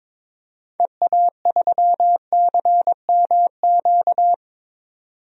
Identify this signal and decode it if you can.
Morse code, heard as a keyed tone, about 21 wpm, 700 Hz. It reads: EA3CMQ
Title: DXCC-100 Call Signs - 6 Characters - 22wpm